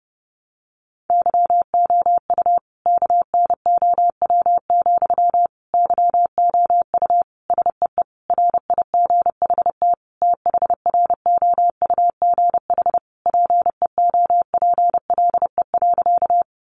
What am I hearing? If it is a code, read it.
Morse code, heard as a keyed tone, about 30 wpm, 700 Hz. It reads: YOU KNOW, YOU SEE RIGHT THROUGH PEOPLE.
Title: Sentences from Top 200 Words 30wpm